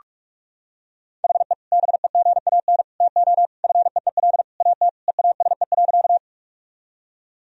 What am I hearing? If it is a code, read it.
Morse code, heard as a keyed tone, about 47 wpm, 700 Hz. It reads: HE BEGAN TO FEEL AT EASE.
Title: Sentences from Top 1000 Words - Review Part 1 - 45wpm